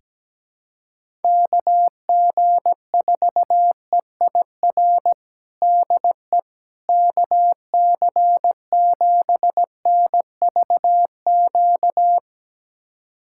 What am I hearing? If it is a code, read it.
Morse code, heard as a keyed tone, about 17 wpm, 700 Hz. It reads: KG4EIR DE KC7NVQ
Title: USA Callsign DE Callsign 17wpm